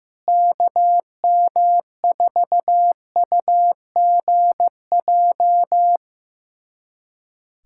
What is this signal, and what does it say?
Morse code, heard as a keyed tone, about 15 wpm, 700 Hz. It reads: KM4UGJ
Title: USA Callsign DE Callsign - 6 Characters - 15wpm